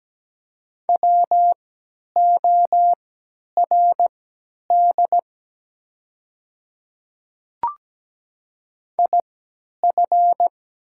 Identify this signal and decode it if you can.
Morse code, heard as a keyed tone, about 17 wpm, spaced as for 10 wpm, 690 Hz. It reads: WORD IF
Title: 3x Character Spacing - Top 100 Words - 17wpm